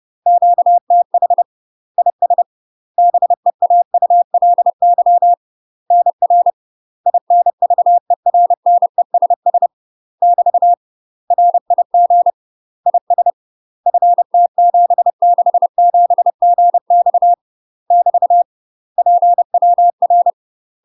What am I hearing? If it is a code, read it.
Morse code, heard as a keyed tone, about 30 wpm, 700 Hz. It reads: QTH IS BEAULY NR INVERNESS <BT> RIG IS FT767GX <BT> PWR
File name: UK MASTER QSOs TRAINING / 30WPM / UK MASTER QSOs 0015